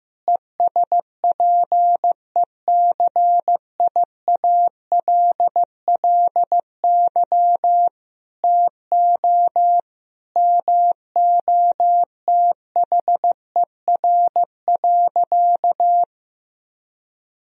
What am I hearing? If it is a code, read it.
Morse code, heard as a keyed tone, about 15 wpm, 700 Hz. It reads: ESPECIALLY TO MOTHER.
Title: Sentences from Top 1000 Words - Review Part 4 - 15wpm